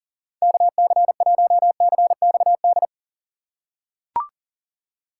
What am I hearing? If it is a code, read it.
Morse code, heard as a keyed tone, about 40 wpm, 700 Hz. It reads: KC1CXD
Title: USA Call Signs - 6 Characters - 4x Character Spacing - 40wpm